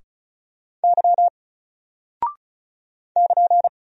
Morse code, heard as a keyed tone, about 36 wpm, 700 Hz. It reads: Y <KN>
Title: Single Character (Letter:Number:Punctuation:Prosign) 35wpm